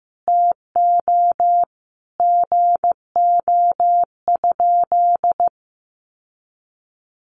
Morse code, heard as a keyed tone, about 15 wpm, 700 Hz. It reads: TO GO?
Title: Sentences from Top 1300 Words 15wpm